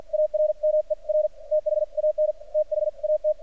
Morse code, heard as a keyed tone, about 46 wpm, spaced as for 40 wpm, 610 Hz. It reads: ARMER THAN THAT
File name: archived / CRSnet